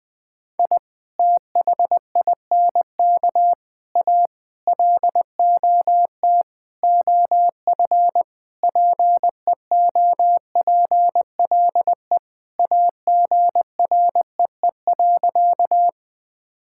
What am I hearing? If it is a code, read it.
Morse code, heard as a keyed tone, about 21 wpm, 700 Hz. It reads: I THINK A LOT OF PEOPLE AGREE.
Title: Sentences with 5-Word Phrases - Spoken Beginning - 20wpm